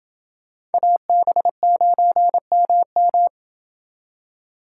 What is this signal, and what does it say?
Morse code, heard as a keyed tone, about 30 wpm, 700 Hz. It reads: AB9MM